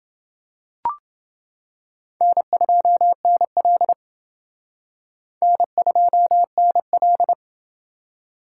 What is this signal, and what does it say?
Morse code, heard as a keyed tone, about 29 wpm, 700 Hz. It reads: N2NL N2NL